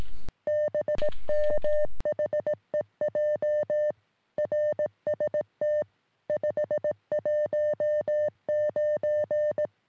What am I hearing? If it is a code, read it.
Morse code, heard as a keyed tone, about 18 wpm, 600 Hz. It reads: BM HEJ RST 519
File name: sm7lcb / ok2poi 3cm